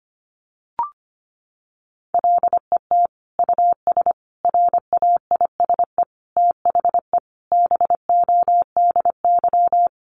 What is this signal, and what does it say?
Morse code, heard as a keyed tone, about 25 wpm, 700 Hz. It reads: LET US RAISE THE BODY
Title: Sentences from Top 1400 Words 25wpm